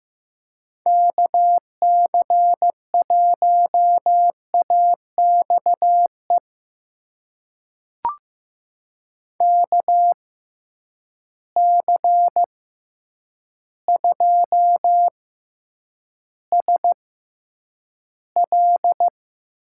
Morse code, heard as a keyed tone, about 15 wpm, spaced as for 8 wpm, 700 Hz. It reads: KC1AXE KC2SL
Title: USA Call Signs - 6 Characters - 6x Character Spacing - 15wpm